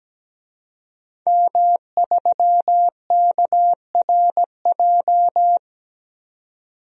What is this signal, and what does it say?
Morse code, heard as a keyed tone, about 17 wpm, 700 Hz. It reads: M3KRJ
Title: DXCC-100 Call Signs - 5 Characters - 17wpm